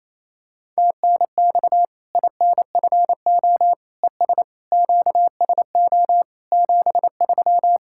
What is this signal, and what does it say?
Morse code, heard as a keyed tone, about 28 wpm, 700 Hz. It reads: TNX INFO ES QSO 73
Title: Practice QSO - Exchange 4 - Ending - 28wpm